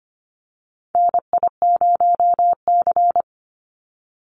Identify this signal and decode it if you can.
Morse code, heard as a keyed tone, about 25 wpm, 700 Hz. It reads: NI0C
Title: Call Signs Repeated 2x - 25wpm